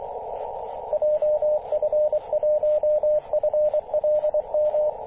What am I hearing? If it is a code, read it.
Morse code, heard as a keyed tone, about 24 wpm, 600 Hz, with radio noise. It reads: JF1FLM
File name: t88rf-21-022mhz-cw